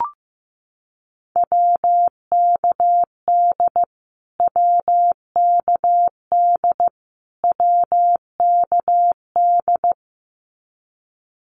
Morse code, heard as a keyed tone, about 15 wpm, 700 Hz. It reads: WKD WKD WKD
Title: Abbreviations Repeated 3x - 15wpm